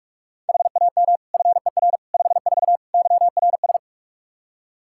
Morse code, heard as a keyed tone, about 46 wpm, 700 Hz. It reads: HAM FER 54 YRS
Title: Practice QSO - Exchange 3 - More Info - 45wpm